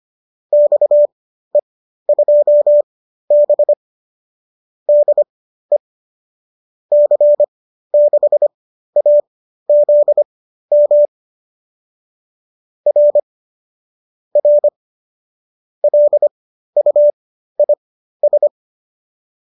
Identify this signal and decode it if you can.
Morse code, heard as a keyed tone, about 25 wpm, spaced as for 13 wpm, 570 Hz. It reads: XE2B DE C6AZM R R LUIS